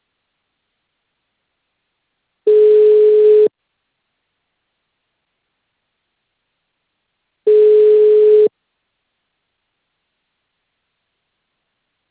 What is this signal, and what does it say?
Ringback tone (Europe, 425 Hz).